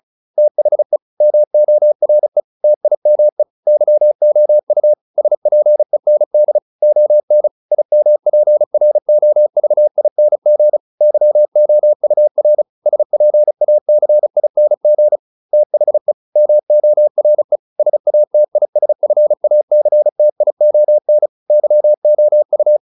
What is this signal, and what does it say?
Morse code, heard as a keyed tone, about 34 wpm, 600 Hz. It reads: THE MORE TIME YOU SPEND ON IMPROVING YOUR SPACING THE MORE SATISFACTION YOU